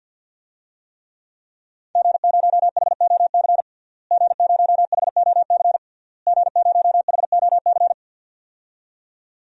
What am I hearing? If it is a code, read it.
Morse code, heard as a keyed tone, about 45 wpm, 700 Hz. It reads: G0HOC G0HOC G0HOC
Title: DXCC-100 Call Signs - 5 Characters - Repeated 3x - 50wpm